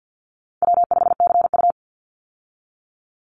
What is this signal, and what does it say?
Morse code, heard as a keyed tone, about 50 wpm, 700 Hz. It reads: W5CU
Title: Easy Call Signs Repeated 2x - 50wpm